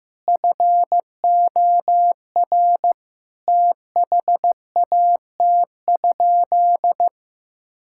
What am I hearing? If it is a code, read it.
Morse code, heard as a keyed tone, about 16 wpm, 700 Hz. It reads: FOR THAT?
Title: Sentences from Top 1000 Words - Review Part 5 - 15wpm